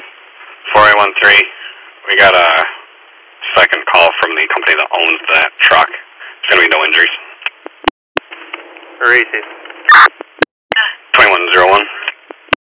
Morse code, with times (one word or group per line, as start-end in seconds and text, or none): none